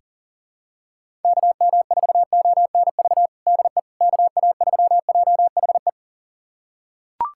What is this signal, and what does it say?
Morse code, heard as a keyed tone, about 37 wpm, 700 Hz. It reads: KM4ONV DE KA3JHE
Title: USA Callsign DE Callsign - 6 Characters - 40wpm